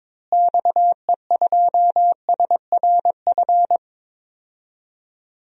Morse code, heard as a keyed tone, about 22 wpm, 700 Hz. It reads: XE2SRF
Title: DXCC-100 Call Signs - 6 Characters - Repeated 3x - 22wpm